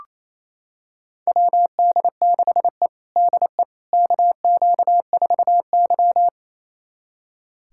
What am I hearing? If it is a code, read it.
Morse code, heard as a keyed tone, about 28 wpm, 700 Hz. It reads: WD6E DE KQ4Y
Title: USA Callsign DE Callsign - 4 Characters - 28wpm